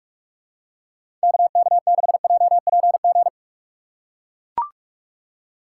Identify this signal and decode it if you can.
Morse code, heard as a keyed tone, about 44 wpm, 700 Hz. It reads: KK6JPG